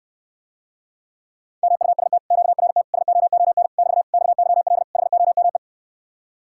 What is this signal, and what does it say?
Morse code, heard as a keyed tone, about 69 wpm, 700 Hz. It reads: MUST QRT SOON <BT> XYL HOME